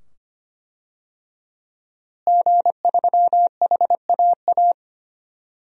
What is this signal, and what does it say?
Morse code, heard as a keyed tone, about 25 wpm, 700 Hz. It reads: G3HAA
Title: DXCC-100 Call Signs - 5 Characters - Repeated 2x - 25wpm